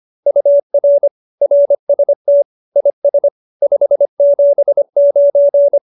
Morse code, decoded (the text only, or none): UR RST IS 579